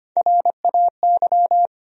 Morse code, heard as a keyed tone, about 25 wpm, 700 Hz. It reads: RAY
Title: USA Common Names - Max 3 Letters - 25wpm